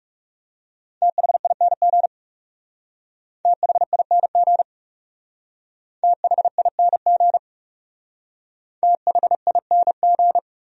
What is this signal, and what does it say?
Morse code, heard as a keyed tone, about 40 wpm, 700 Hz. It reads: THING THING THING THING